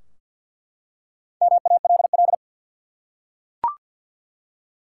Morse code, heard as a keyed tone, about 47 wpm, 700 Hz. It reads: MALL
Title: Four Letter Words - 50wpm